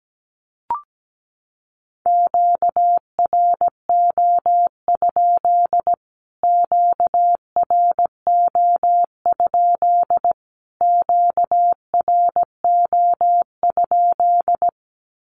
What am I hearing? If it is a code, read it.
Morse code, heard as a keyed tone, about 17 wpm, 700 Hz. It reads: QRO? QRO? QRO?
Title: Q-Codes Repeated 3x 17wpm